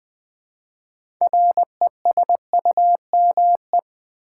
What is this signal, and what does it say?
Morse code, heard as a keyed tone, about 20 wpm, 700 Hz. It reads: RESUME
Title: Build A Word - 2 Letters At A Time - 6-Letter Words - 20wpm